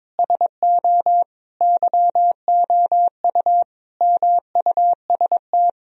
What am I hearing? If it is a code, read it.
Morse code, heard as a keyed tone, about 22 wpm, 700 Hz. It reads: SO YOU MUST
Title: Sentences from Top 1000 Words - Review Part 2 - 22wpm